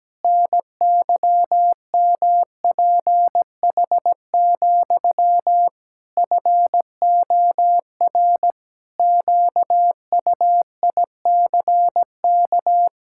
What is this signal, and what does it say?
Morse code, heard as a keyed tone, about 17 wpm, 700 Hz. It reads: NYMPH, FOR QUICK
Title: Pangram Challenge v2 - 17wpm